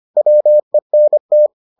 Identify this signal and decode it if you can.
Morse code, heard as a keyed tone, about 25 wpm, 600 Hz. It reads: WENT